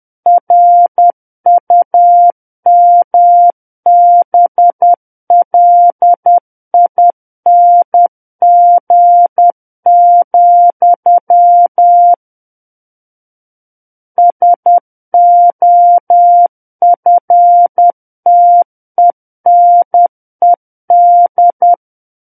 Morse code, heard as a keyed tone, about 10 wpm, 700 Hz. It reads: RUMBLING, SOFTENED
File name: Book One / 01-Chapter 1 - 10 WPM